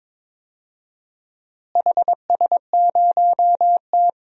Morse code, heard as a keyed tone, about 22 wpm, 700 Hz. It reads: HS0T
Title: DXCC-100 Call Signs - 4 Characters - Repeated 3x - 22wpm